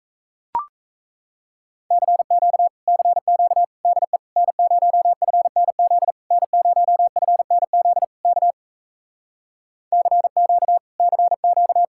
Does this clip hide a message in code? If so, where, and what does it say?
Yes: Morse code from 1.9 to 11.8 s, heard as a keyed tone, about 38 wpm, 700 Hz. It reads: CQ CQ DE N0FNZ N0FNZ K CQ CQ